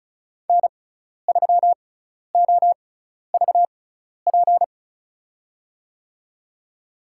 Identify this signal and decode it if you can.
Morse code, heard as a keyed tone, about 36 wpm, spaced as for 12 wpm, 700 Hz. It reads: N3OVP
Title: USA Call Signs - 5 Characters - 6x Character Spacing - 35wpm